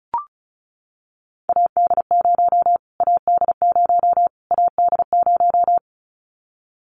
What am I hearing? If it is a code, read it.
Morse code, heard as a keyed tone, about 34 wpm, 700 Hz. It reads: AD0 AD0 AD0
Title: Callsign Prefixes for Top 25 DXCC - Repeated 3x - 35wpm